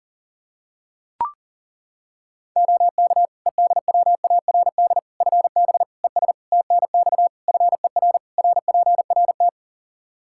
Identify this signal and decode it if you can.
Morse code, heard as a keyed tone, about 40 wpm, 700 Hz. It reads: OK EDWARD FB ES TNX FER RPRT
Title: Practice QSO - Exchange 2 - Info - 40wpm